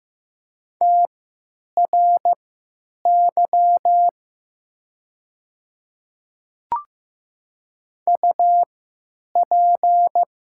Morse code, heard as a keyed tone, about 15 wpm, spaced as for 9 wpm, 700 Hz. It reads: TRY UP
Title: 3x Character Spacing - Top 300 Words - 15wpm